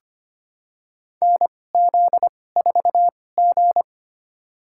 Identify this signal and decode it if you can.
Morse code, heard as a keyed tone, about 25 wpm, spaced as for 18 wpm, 700 Hz. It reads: NZ4G
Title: USA Call Signs - 2x Character Spacing - 25wpm